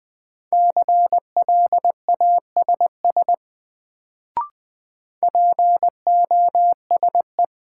Morse code, heard as a keyed tone, about 20 wpm, 700 Hz. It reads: CLASS POSE
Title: Sentences from Top 1000 Words - Review Part 2 - New Word Drill - 20wpm